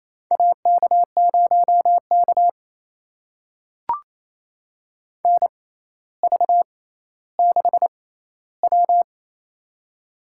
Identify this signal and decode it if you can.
Morse code, heard as a keyed tone, about 28 wpm, spaced as for 14 wpm, 700 Hz. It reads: AK0K NV6W
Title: USA Call Signs - 4 Characters - 6x Character Spacing - 28wpm